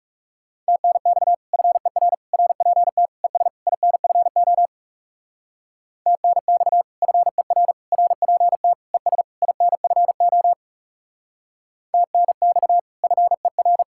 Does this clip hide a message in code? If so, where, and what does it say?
Yes: Morse code from 0.7 to 13.8 s, heard as a keyed tone, about 38 wpm, 700 Hz. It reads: TNX FER RPT ES INFO TNX FER RPT ES INFO TNX FER